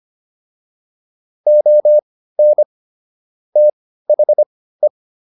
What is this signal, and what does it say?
Morse code, heard as a keyed tone, about 25 wpm, spaced as for 15 wpm, 600 Hz. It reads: ON THE
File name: WD305 15